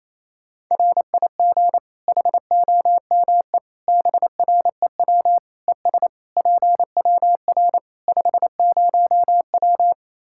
Morse code, heard as a keyed tone, about 28 wpm, 700 Hz. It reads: RIG HOME BREW ES PWR 50W
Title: K1USN SST QSO Practice 28wpm